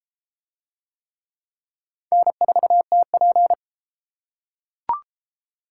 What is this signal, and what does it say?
Morse code, heard as a keyed tone, about 30 wpm, 700 Hz. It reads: N4TP